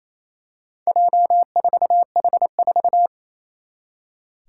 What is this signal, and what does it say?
Morse code, heard as a keyed tone, about 28 wpm, 700 Hz. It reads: J4H4
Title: Four Challenging Letter-Number Combinations 28wpm